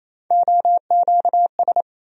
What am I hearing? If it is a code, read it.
Morse code, heard as a keyed tone, about 28 wpm, 700 Hz. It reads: OQS
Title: USA Call Sign Postfixes - 3 Letters 28wpm